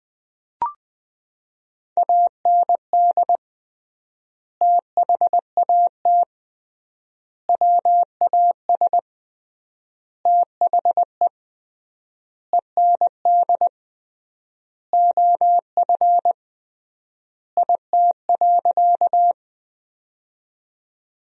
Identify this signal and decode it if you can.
Morse code, heard as a keyed tone, about 21 wpm, 700 Hz. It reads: AND THAT WAS THE END OF IT.
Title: 3x Word-Spacing - Sentences from Top 200 Words - 20wpm